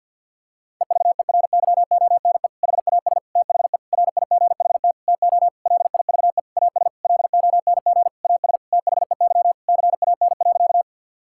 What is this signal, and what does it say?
Morse code, heard as a keyed tone, about 50 wpm, 700 Hz. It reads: EVERYONE HAS THE RIGHT TO LIVE AS LONG AS THEY CAN.
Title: Sentences from Top 1000 Words - Review Part 5 - 50wpm